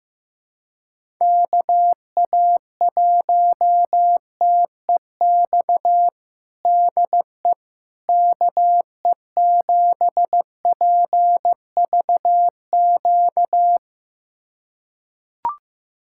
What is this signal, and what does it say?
Morse code, heard as a keyed tone, about 15 wpm, 700 Hz. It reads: KA1TEX DE KE7PVQ
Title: USA Callsign DE Callsign 15wpm